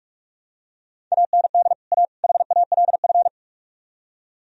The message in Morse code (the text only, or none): AND A HALF